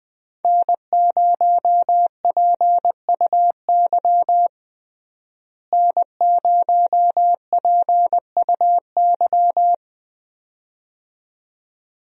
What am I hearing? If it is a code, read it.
Morse code, heard as a keyed tone, about 20 wpm, 700 Hz. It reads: N0PUY N0PUY
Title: USA Call Signs - 5 Characters - Repeated 2x - 20wpm